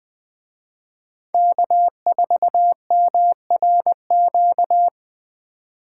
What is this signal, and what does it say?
Morse code, heard as a keyed tone, about 20 wpm, 700 Hz. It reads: K4MRQ
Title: Speed Racing - USA 1x3 Call Signs 17wpm